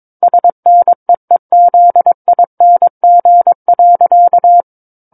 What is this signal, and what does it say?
Morse code, heard as a keyed tone, about 22 wpm, 700 Hz. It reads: SNEEZING.
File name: 06-Chapter-06 22 WPM